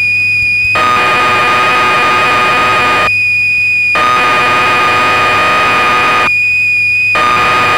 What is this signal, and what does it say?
Cassette data (Kansas City type).
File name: magazines / Acorn User / 1984-11 BBC Programs